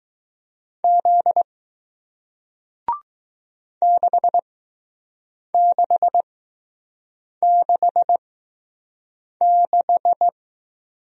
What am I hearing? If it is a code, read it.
Morse code, heard as a keyed tone, about 20 wpm, spaced as for 6 wpm, 700 Hz. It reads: Z 6666